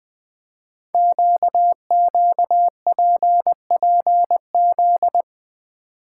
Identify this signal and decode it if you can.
Morse code, heard as a keyed tone, about 20 wpm, 700 Hz. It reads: QQPPZ
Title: Code Groups - 5 Letters - 20wpm